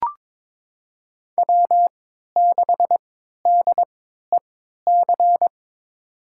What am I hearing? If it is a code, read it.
Morse code, heard as a keyed tone, about 22 wpm, spaced as for 13 wpm, 700 Hz. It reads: W6DEC
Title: USA Call Signs - 5 Characters - 3x Character Spacing - 22wpm